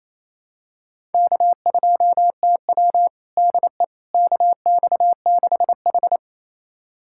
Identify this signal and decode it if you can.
Morse code, heard as a keyed tone, about 28 wpm, 700 Hz. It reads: K2TW DE KX6H
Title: USA Callsign DE Callsign - 4 Characters - 28wpm